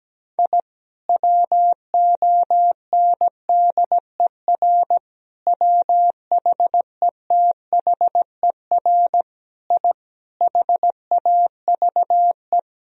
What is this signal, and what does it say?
Morse code, heard as a keyed tone, about 17 wpm, 700 Hz. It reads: I WONDER WHETHER I HAVE
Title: Sentences from Top 1000 Words - Review Part 3 - 17wpm